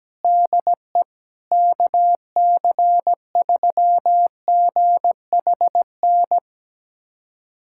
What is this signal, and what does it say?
Morse code, heard as a keyed tone, about 17 wpm, 700 Hz. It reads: DE KC3GHN
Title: USA Callsign DE Callsign - 6 Characters - 17wpm